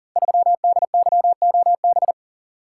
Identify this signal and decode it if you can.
Morse code, heard as a keyed tone, about 39 wpm, 700 Hz. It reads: 3DYOB
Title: Code Groups - 5 Letter:Numbers - 40wpm